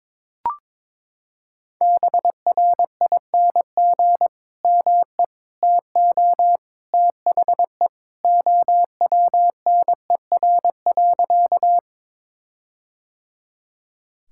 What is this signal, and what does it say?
Morse code, heard as a keyed tone, about 22 wpm, 700 Hz. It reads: BRING ME TO THE OWNER.
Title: Sentences from Top 1600 Words 22wpm